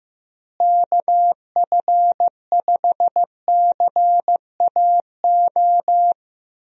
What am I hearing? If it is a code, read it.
Morse code, heard as a keyed tone, about 15 wpm, 700 Hz. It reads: KF5CAO
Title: USA Call Signs - 6 Characters - Repeated 3x - 15wpm